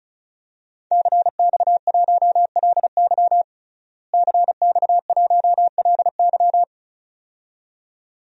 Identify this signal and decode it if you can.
Morse code, heard as a keyed tone, about 36 wpm, 700 Hz. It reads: CX1LY CX1LY
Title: DXCC-100 Call Signs - 5 Characters - Repeated 2x - 35wpm